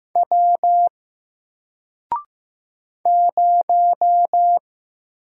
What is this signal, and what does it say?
Morse code, heard as a keyed tone, about 15 wpm, 700 Hz. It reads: W 0
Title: Single Letter-Number Alt 15wpm